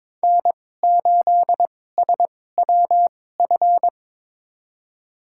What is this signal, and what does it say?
Morse code, heard as a keyed tone, about 22 wpm, spaced as for 16 wpm, 700 Hz. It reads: N8SWF
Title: USA Call Signs - 5 Characters - 2x Character Spacing - 22wpm